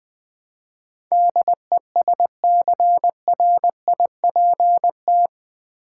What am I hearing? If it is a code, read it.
Morse code, heard as a keyed tone, about 20 wpm, 700 Hz. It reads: DESCRIPT
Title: Build A Word - 4 Letters At A Time - 11-Letter Words - 20wpm